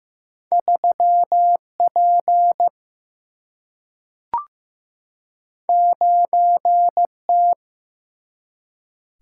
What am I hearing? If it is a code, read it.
Morse code, heard as a keyed tone, about 15 wpm, 700 Hz. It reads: 3P 9T
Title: Single-Letter Single-Number 15wpm